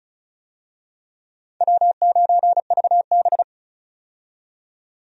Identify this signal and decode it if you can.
Morse code, heard as a keyed tone, about 36 wpm, 700 Hz. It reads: W9VB
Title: USA Call Signs - 4 Characters - 35wpm